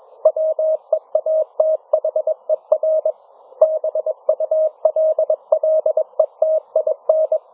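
Morse code, heard as a keyed tone, about 21 wpm, 630 Hz. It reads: WEATHER BULLETIN
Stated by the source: Morse challenge from NZ Net newsletter; NZ Net, national amateur radio net of New Zealand, meets on 3535.0 kHz.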